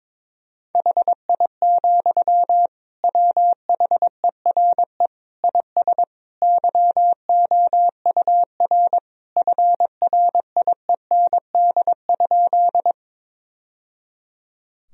Morse code, heard as a keyed tone, about 21 wpm, 700 Hz. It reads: HI, WHERE IS YOUR FRIEND?
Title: Sentences from Top 1000 Words - Review Part 3 - 22wpm